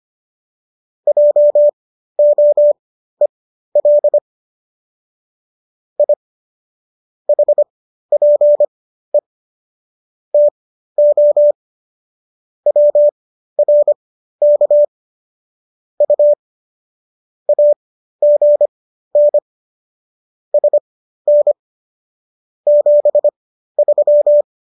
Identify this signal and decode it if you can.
Morse code, heard as a keyed tone, about 25 wpm, spaced as for 13 wpm, 590 Hz. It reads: JOEL I HPE TO WRK U AGN SN 73
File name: qso214 13